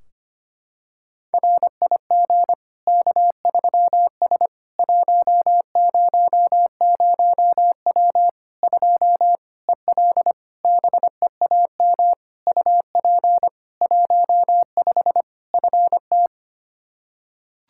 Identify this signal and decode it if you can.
Morse code, heard as a keyed tone, about 25 wpm, 700 Hz. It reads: RIG K3S 100W 2 EL BEAM UP 15 FT
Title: Practice QSO - Exchange 2 - Info - 25wpm